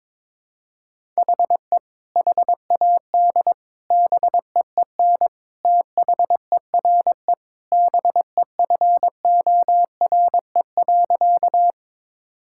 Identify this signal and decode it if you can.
Morse code, heard as a keyed tone, about 22 wpm, 700 Hz. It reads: HE HAD BEEN THERE BEFORE.
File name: practice / 22wpm / Sentences with 3-Word Phrases - 22wpm